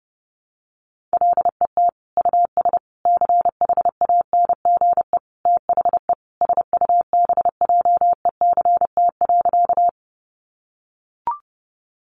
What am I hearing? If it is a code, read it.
Morse code, heard as a keyed tone, about 30 wpm, 700 Hz. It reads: LET US CHANGE THE SUBJECT.
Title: Sentences from Top 800 Words 30wpm